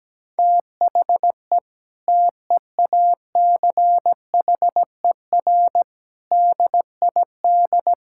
Morse code, heard as a keyed tone, about 17 wpm, 700 Hz. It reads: THE TEACHER DID
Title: Sentences from Top 1000 Words - Review Part 6 - 17wpm